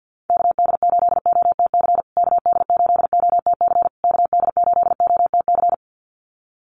Morse code, heard as a keyed tone, about 50 wpm, 700 Hz. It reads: KD8OTC KD8OTC KD8OTC
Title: USA Call Signs Repeated 3x 50wpm